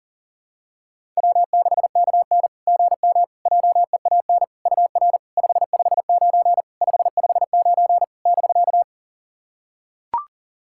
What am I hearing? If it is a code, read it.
Morse code, heard as a keyed tone, about 44 wpm, 700 Hz. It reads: W6KN GM JEAN UR 559 559 <BK>
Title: SOTA QSO Practice 40wpm